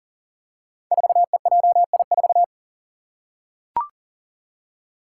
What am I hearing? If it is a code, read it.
Morse code, heard as a keyed tone, about 39 wpm, 700 Hz. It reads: 4EJI4
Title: Five Challenging Letter-Number Combinations 40wpm